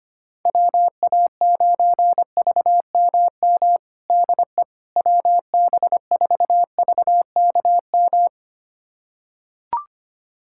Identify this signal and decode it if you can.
Morse code, heard as a keyed tone, about 25 wpm, 700 Hz. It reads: WA9VMM DE WB4VKM
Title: USA Callsign DE Callsign - 6 Characters - 25wpm